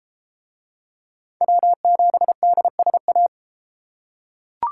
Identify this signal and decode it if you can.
Morse code, heard as a keyed tone, about 32 wpm, 700 Hz. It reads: W7DSA